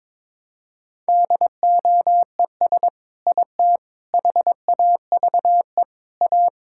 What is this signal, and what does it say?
Morse code, heard as a keyed tone, about 22 wpm, 700 Hz. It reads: DOES IT HAVE A
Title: Sentences from Top 1000 Words - Review Part 5 - 22wpm